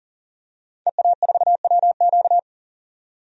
Morse code, heard as a keyed tone, about 40 wpm, 700 Hz. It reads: EA4WQ
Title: DXCC-100 Call Signs - 5 Characters - 40wpm